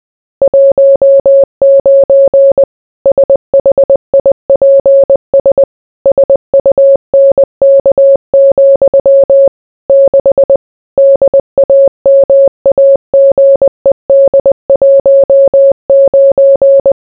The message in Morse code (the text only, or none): 19 SHIPS SUNK, 6 DAMAGED 19